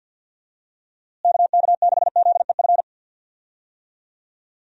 Morse code, heard as a keyed tone, about 52 wpm, 700 Hz. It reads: KK6ZEF
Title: USA Call Signs - 6 Characters - Repeated 3x - 50wpm